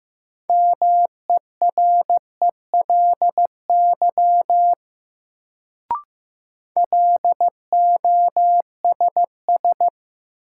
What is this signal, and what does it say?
Morse code, heard as a keyed tone, about 15 wpm, 700 Hz. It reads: MERELY LOSS
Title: Sentences from Top 1400 Words - New Word Drill - 15wpm